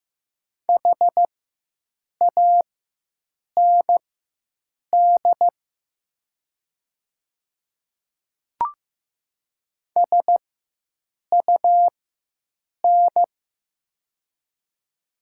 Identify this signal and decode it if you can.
Morse code, heard as a keyed tone, about 15 wpm, spaced as for 7 wpm, 700 Hz. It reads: HAND SUN
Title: 4x Character Spacing - Top 300 Words - 15wpm